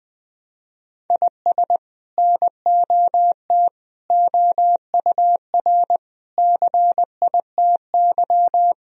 Morse code, heard as a keyed tone, about 20 wpm, 700 Hz. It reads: IS NOT OUR CITY
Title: Sentences from Top 1000 Words - Review Part 3 - 20wpm